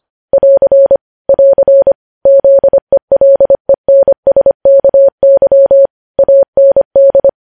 Morse code, heard as a keyed tone, about 25 wpm, 550 Hz. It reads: <AR> <AR> ZELENSKY AND
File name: podcast / News headlines-1x-25x25